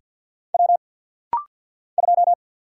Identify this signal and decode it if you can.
Morse code, heard as a keyed tone, about 50 wpm, 700 Hz. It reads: W 2
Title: Single Letter-Number - Rapid-Fire 50wpm